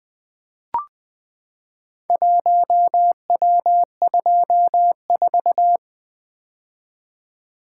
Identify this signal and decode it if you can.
Morse code, heard as a keyed tone, about 20 wpm, 700 Hz. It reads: 1W24
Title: Four Letter-Number Combinations - 20wpm